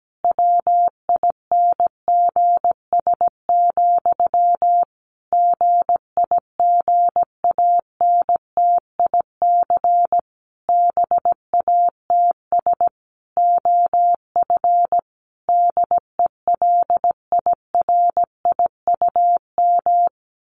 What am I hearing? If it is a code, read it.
Morse code, heard as a keyed tone, about 17 wpm, 700 Hz. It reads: WINGS, GIGANTIC BATS OF DELIRIUM
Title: A Sound Of Thunder 17wpm